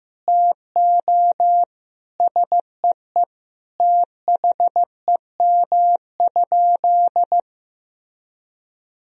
Morse code, heard as a keyed tone, about 15 wpm, 700 Hz. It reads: TO SEE THEM?
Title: Sentences from Top 1000 Words - Review Part 1 - 15wpm